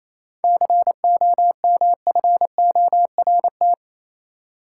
Morse code, heard as a keyed tone, about 28 wpm, 700 Hz. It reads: COMFORT
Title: Seven Letter Words - 28wpm